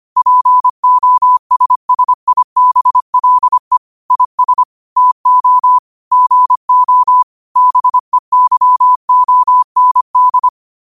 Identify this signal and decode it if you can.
Morse code, heard as a keyed tone, about 25 wpm, 1000 Hz. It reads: POSSIBLE IS TO GO BEYOND
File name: 25wpm